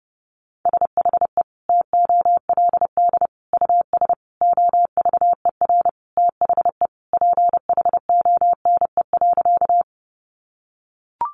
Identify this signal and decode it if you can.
Morse code, heard as a keyed tone, about 30 wpm, 700 Hz. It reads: SHE TOLD US OVER THE PHONE.
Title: Sentences from Top 1100 Words 30wpm